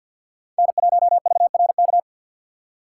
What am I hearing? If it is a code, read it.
Morse code, heard as a keyed tone, about 51 wpm, 700 Hz. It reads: N1VRK
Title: USA Call Signs - 5 Characters - 6x Character Spacing - 50wpm